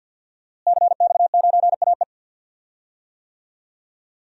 Morse code, heard as a keyed tone, about 52 wpm, 700 Hz. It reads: CX9AE
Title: DXCC-100 Call Signs - 5 Characters - Repeated 2x - 50wpm